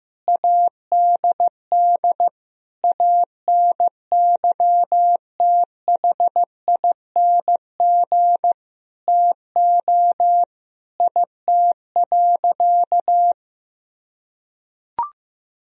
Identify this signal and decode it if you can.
Morse code, heard as a keyed tone, about 15 wpm, 700 Hz. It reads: ADD ANYTHING TO IT.
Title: Sentences from Top 1000 Words - Review Part 1 - 15wpm